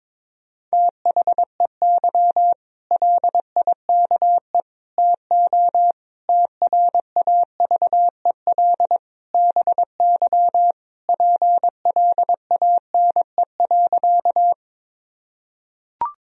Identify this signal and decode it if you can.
Morse code, heard as a keyed tone, about 22 wpm, 700 Hz. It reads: THEY LIKE TO TRAVEL BY PLANE.
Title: Sentences from Top 1000 Words - Review Part 3 - 22wpm